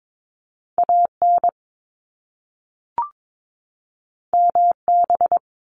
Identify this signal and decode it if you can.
Morse code, heard as a keyed tone, about 22 wpm, 700 Hz. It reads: AN MB
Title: Top Two Letter Combinations in Words 22wpm